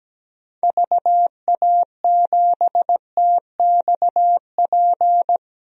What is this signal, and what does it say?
Morse code, heard as a keyed tone, about 17 wpm, 700 Hz. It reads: VA7TXP
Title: DXCC-100 Call Signs - 6 Characters - Repeated 2x - 17wpm